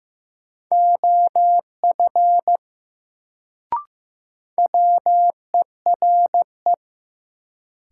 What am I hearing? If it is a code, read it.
Morse code, heard as a keyed tone, about 15 wpm, 700 Hz. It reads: OF WERE
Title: Sets of 2 Words - New Word Drill - 15wpm